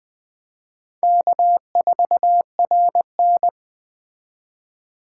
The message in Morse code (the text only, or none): K4RN